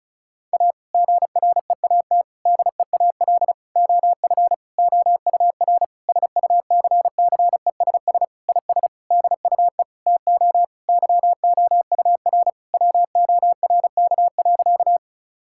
Morse code, heard as a keyed tone, about 36 wpm, 700 Hz. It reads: A GREAT DEAL OF OUR SUCCESS IS DUE TO YOUR WORK.
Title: Sentences from Top 1000 Words - Review Part 6 - 35wpm